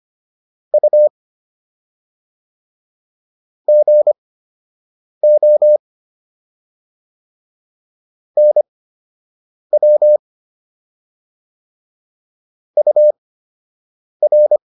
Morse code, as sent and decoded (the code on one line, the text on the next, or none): ..- / --. --- / -. .-- / ..- .-.
U GO NW UR